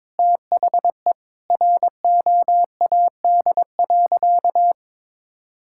Morse code, heard as a keyed tone, about 22 wpm, 700 Hz. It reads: THE ROAD.
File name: practice / 22wpm / Sentences with 4-Word Phrases - 22wpm